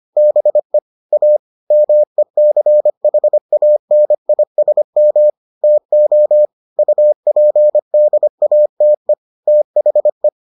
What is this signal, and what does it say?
Morse code, heard as a keyed tone, about 25 wpm, 600 Hz. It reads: BE A MECHANISM TO UPDATE THE